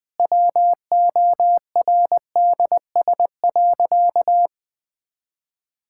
Morse code, heard as a keyed tone, about 20 wpm, 700 Hz. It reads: WORDS.
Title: Difficult Sentences with Hints - 20wpm